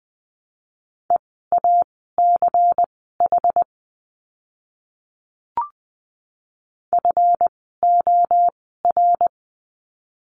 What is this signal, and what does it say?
Morse code, heard as a keyed tone, about 20 wpm, spaced as for 14 wpm, 700 Hz. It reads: EACH FOR
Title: 192 Lesson 31f - Letter v - Words 20wpm